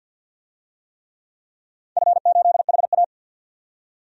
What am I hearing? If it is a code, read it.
Morse code, heard as a keyed tone, about 47 wpm, 700 Hz. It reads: F8HA
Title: DXCC-100 Call Signs - 4 Characters - 50wpm